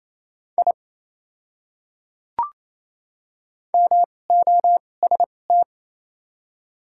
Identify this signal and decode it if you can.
Morse code, heard as a keyed tone, about 28 wpm, spaced as for 20 wpm, 700 Hz. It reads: I MOST
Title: 2x Character Spacing - Top 100 Words - 28wpm